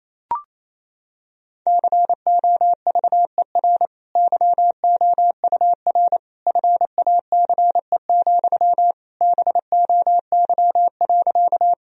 Morse code, heard as a keyed tone, about 27 wpm, 700 Hz. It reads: COVER YOUR FACE, BOY.
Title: Sentences from Top 1000 Words - Review Part 1 - 28wpm